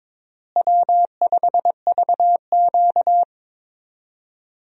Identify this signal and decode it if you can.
Morse code, heard as a keyed tone, about 23 wpm, 700 Hz. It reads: W5VQ
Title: USA Call Signs - 4 Characters - Repeated 3x - 22wpm